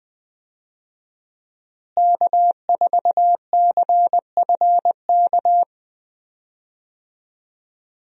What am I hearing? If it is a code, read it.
Morse code, heard as a keyed tone, about 20 wpm, 700 Hz. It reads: K4CFK
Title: USA Call Signs - 5 Characters - Repeated 3x - 20wpm